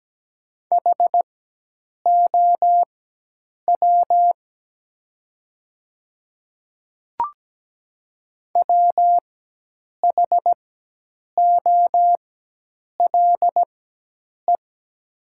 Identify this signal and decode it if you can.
Morse code, heard as a keyed tone, about 17 wpm, spaced as for 8 wpm, 700 Hz. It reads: HOW WHOLE
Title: 4x Character Spacing - Top 500 Words - 17wpm